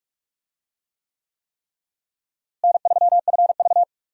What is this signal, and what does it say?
Morse code, heard as a keyed tone, about 45 wpm, 700 Hz. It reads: N3FV
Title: USA Call Signs - 4 Characters - Repeated 3x - 45wpm